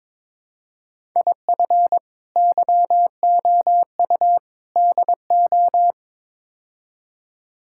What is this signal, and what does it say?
Morse code, heard as a keyed tone, about 22 wpm, 700 Hz. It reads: IF YOU DO
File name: practice / 22wpm / Build A Phrase - 5-Word Phrases - 22wpm